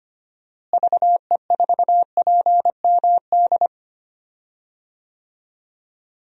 Morse code, heard as a keyed tone, about 25 wpm, 700 Hz. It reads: VE4PMD
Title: DXCC-100 Call Signs - 6 Characters - Repeated 3x - 25wpm